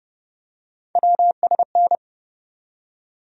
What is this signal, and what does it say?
Morse code, heard as a keyed tone, about 31 wpm, 700 Hz. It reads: WSN
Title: USA Call Sign Postfixes - 3 Letters 30wpm